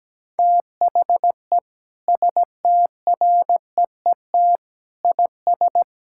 Morse code, heard as a keyed tone, about 17 wpm, 700 Hz. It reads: THE STREET IS
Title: Sentences from Top 1600 Words 17wpm